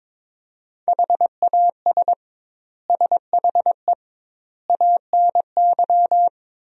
Morse code, heard as a keyed tone, about 21 wpm, 700 Hz. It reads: HAS SHE ANY
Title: 2x Word-Spacing - Sentences from Top 400 Words - 22wpm